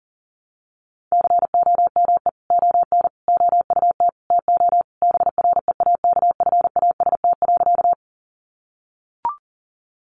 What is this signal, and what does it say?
Morse code, heard as a keyed tone, about 40 wpm, 700 Hz. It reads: COME ON OUT TO BREAKFAST.
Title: Sentences from Top 1500 Words 40wpm